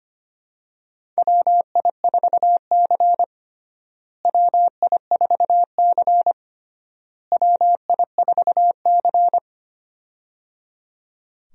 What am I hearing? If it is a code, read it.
Morse code, heard as a keyed tone, about 25 wpm, 700 Hz. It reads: WI4C WI4C WI4C
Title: USA Call Signs - 4 Characters - Repeated 3x - 25wpm